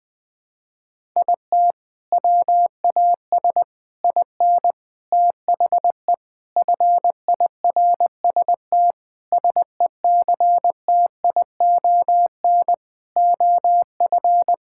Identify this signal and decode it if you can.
Morse code, heard as a keyed tone, about 20 wpm, 700 Hz. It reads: IT WAS IN THE FIRST SECTION OF
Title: Sentences from Top 1000 Words - Review Part 5 - 20wpm